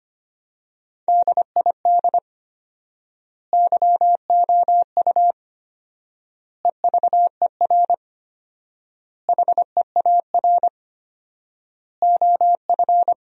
Morse code, heard as a keyed tone, about 25 wpm, 700 Hz. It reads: DID YOU EVER HEAR OF
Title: 4x Word-Spacing - Sentences from Top 400 Words - 25wpm